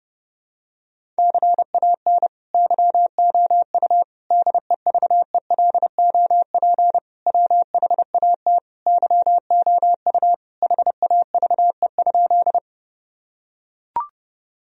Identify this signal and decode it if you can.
Morse code, heard as a keyed tone, about 30 wpm, 700 Hz. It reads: CAN YOU DEVELOP WHAT YOU HAVE?
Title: Sentences from Top 1000 Words - Review Part 2 - 30wpm